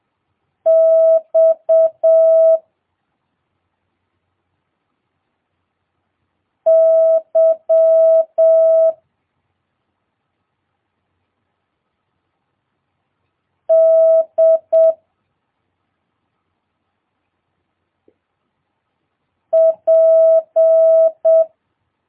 Morse code, heard as a keyed tone, about 7 wpm, 650 Hz. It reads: X Y D P
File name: jh8cbh / 2022cw